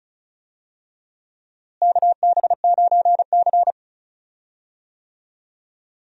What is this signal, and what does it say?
Morse code, heard as a keyed tone, about 36 wpm, 700 Hz. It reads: KB9C
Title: USA Call Signs - 4 Characters - Repeated 3x - 35wpm